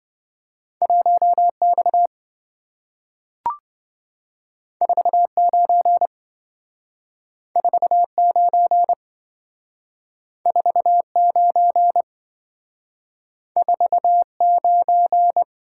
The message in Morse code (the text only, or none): 1X 49 49 49 49